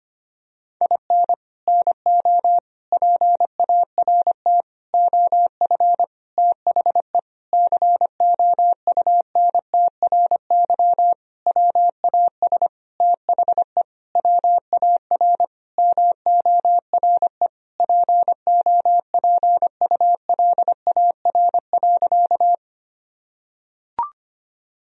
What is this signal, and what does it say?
Morse code, heard as a keyed tone, about 25 wpm, 700 Hz. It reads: IN NO PART OF THE COUNTRY WAS THE WAR MORE POPULAR.
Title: Sentences from Top 1000 Words - Review Part 3 - 25wpm